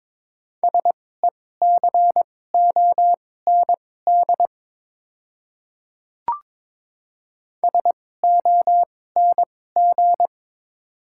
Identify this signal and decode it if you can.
Morse code, heard as a keyed tone, about 22 wpm, spaced as for 16 wpm, 700 Hz. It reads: SECOND SONG
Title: 2x Character Spacing - Top 500 Words - 22wpm